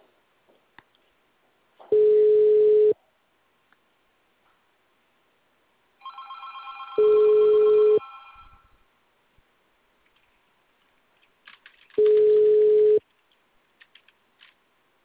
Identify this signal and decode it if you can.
Ringback tone (Europe, 425 Hz).